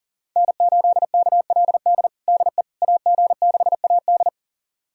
Morse code, heard as a keyed tone, about 37 wpm, 700 Hz. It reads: N8KLD DE AG6AD
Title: USA Callsign DE Callsign - 5 Characters - 40wpm